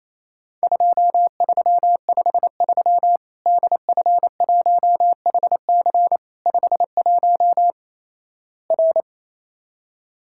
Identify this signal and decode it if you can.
Morse code, heard as a keyed tone, about 28 wpm, 700 Hz. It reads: 2353 DF1HC 51 R
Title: Worked All Europe DX Contest - QTC Exchange - 28wpm